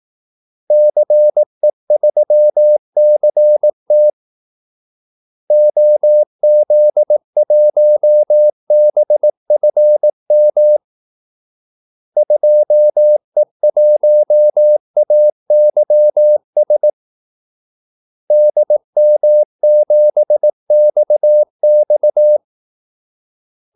Morse code, heard as a keyed tone, about 18 wpm, 600 Hz. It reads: CE3CT OZ1BFM 2E1AYS DM7XX
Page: Kaldesignaler 18 wpm